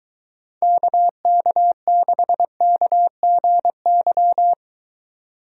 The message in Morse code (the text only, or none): KK6KGY